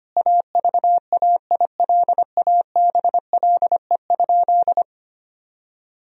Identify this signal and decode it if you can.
Morse code, heard as a keyed tone, about 25 wpm, 700 Hz. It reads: AVAILABLE?
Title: Sentences from Top 1000 Words - Review Part 3 - 25wpm